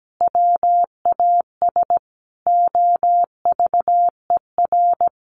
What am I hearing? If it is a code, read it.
Morse code, heard as a keyed tone, about 17 wpm, 700 Hz. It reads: WAS OVER
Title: Sentences from Top 1200 Words 17wpm